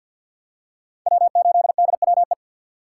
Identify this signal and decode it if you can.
Morse code, heard as a keyed tone, about 51 wpm, 700 Hz. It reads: W8DWE
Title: USA Call Signs - 5 Characters - 3x Character Spacing - 50wpm